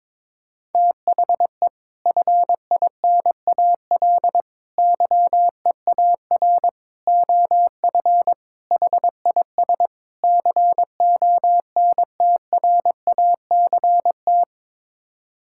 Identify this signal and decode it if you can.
Morse code, heard as a keyed tone, about 22 wpm, 700 Hz. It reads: THE FINAL YEAR OF HIS CONTRACT
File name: practice / 22wpm / Sets of 6 Words - 22wpm